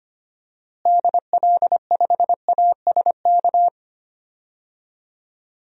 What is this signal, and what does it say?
Morse code, heard as a keyed tone, about 25 wpm, 700 Hz. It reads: DL5ASK
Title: DXCC-100 Call Signs - 6 Characters - Repeated 3x - 25wpm